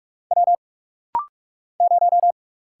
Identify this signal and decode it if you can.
Morse code, heard as a keyed tone, about 40 wpm, 700 Hz. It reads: W 0
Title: Single Letter-Number - Rapid-Fire 45wpm